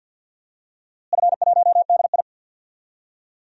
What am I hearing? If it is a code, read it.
Morse code, heard as a keyed tone, about 51 wpm, 700 Hz. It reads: F1DI
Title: DXCC-100 Call Signs - 4 Characters - Repeated 3x - 50wpm